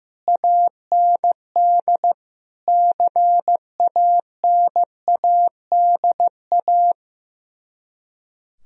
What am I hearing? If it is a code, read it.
Morse code, heard as a keyed tone, about 15 wpm, 700 Hz. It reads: AND CANADA
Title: Speed-Racing - Sets of 5 Words 15wpm